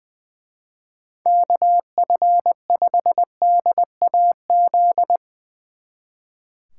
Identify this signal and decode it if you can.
Morse code, heard as a keyed tone, about 22 wpm, 700 Hz. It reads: KF5DAZ
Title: Speed Racing - USA 2x3 Call Signs 20wpm